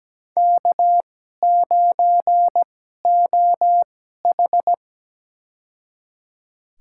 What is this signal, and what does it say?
Morse code, heard as a keyed tone, about 17 wpm, spaced as for 12 wpm, 700 Hz. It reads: K9OH
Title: USA Call Signs - 4 Characters - 2x Character Spacing - 17wpm